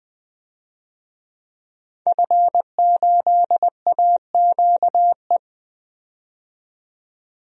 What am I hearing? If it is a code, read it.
Morse code, heard as a keyed tone, about 20 wpm, 700 Hz. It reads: F8AQE
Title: DXCC-100 Call Signs - 5 Characters - 20wpm